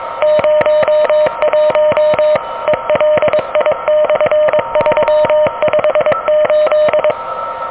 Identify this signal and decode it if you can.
Morse code, heard as a keyed tone, about 22 wpm, 600 Hz, with noise. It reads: 01 ELI/358
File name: OK QRP 2023 / OK QRP 2023